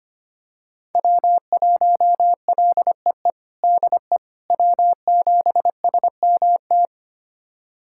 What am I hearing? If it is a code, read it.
Morse code, heard as a keyed tone, about 25 wpm, 700 Hz. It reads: W1LEE DE W7SMT
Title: USA Callsign DE Callsign - 5 Characters - 25wpm